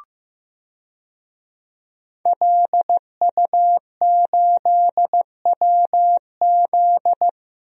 Morse code, heard as a keyed tone, about 15 wpm, 700 Hz. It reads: LU8WZ
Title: DXCC-100 Call Signs - 5 Characters - Repeated 3x - 15wpm